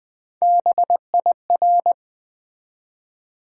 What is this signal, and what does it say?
Morse code, heard as a keyed tone, about 20 wpm, 700 Hz. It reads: BIR
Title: Build A Word - 1 Letter At A Time - 5-Letter Words - 20wpm